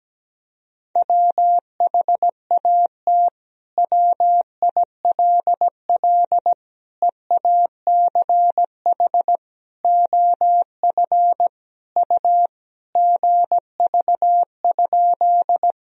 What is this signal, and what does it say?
Morse code, heard as a keyed tone, about 17 wpm, 700 Hz. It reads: WHAT WILL EACH OF U GV?
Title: Sentences from Top 200 Words and Abbreviations 17wpm